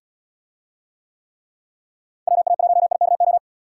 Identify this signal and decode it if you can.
Morse code, heard as a keyed tone, about 77 wpm, 700 Hz. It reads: WA1IKO